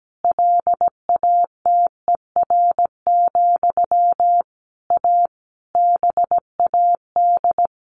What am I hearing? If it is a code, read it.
Morse code, heard as a keyed tone, about 17 wpm, 700 Hz. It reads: LATER, A BAD
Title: A Sound Of Thunder 17wpm